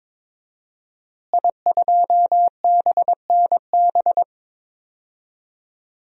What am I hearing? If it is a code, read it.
Morse code, heard as a keyed tone, about 22 wpm, 700 Hz. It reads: I2BNB
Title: DXCC-100 Call Signs - 5 Characters - Repeated 2x - 22wpm